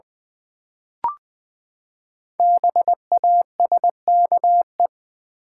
Morse code, heard as a keyed tone, about 20 wpm, 700 Hz. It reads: BASKE
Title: Build A Word - 5 Letters At A Time - 10-Letter Words - 20wpm